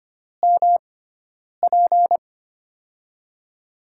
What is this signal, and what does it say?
Morse code, heard as a keyed tone, about 25 wpm, 700 Hz. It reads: M P
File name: Top 90 QSO Elements - 6x Character Spacing - 25wpm